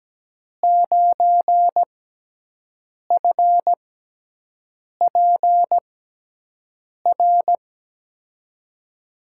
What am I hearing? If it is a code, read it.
Morse code, heard as a keyed tone, about 17 wpm, spaced as for 6 wpm, 700 Hz. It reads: 9FPR
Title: USA Call Signs - 6 Characters - 6x Character Spacing - 17wpm